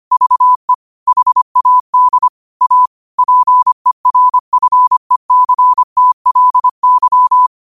Morse code, heard as a keyed tone, about 25 wpm, 1000 Hz. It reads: VE HAD A PERFECTLY
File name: qotdfiles / 25wpm